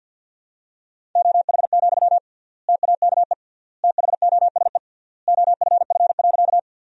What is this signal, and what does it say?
Morse code, heard as a keyed tone, about 52 wpm, 700 Hz. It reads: OH, TAKE THOSE OFF.
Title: 3x Word-Spacing - Sentences from Top 500 Words - 50wpm